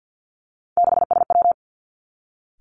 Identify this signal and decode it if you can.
Morse code, heard as a keyed tone, about 50 wpm, 700 Hz. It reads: 6SW
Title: Three Challenging Letter-Number Combinations 50wpm